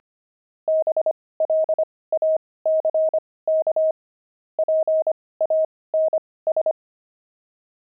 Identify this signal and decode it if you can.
Morse code, heard as a keyed tone, about 25 wpm, spaced as for 18 wpm, 630 Hz. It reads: BLACK PANS
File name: pangram-fs-18-cs-25-twelve